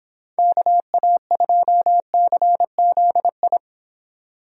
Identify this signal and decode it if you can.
Morse code, heard as a keyed tone, about 30 wpm, 700 Hz. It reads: KA2CZI